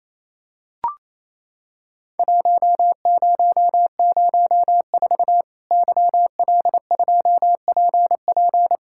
Morse code, heard as a keyed tone, about 28 wpm, 700 Hz. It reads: 1004 YL2PP
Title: Worked All Europe DX Contest - QTC Exchange - 28wpm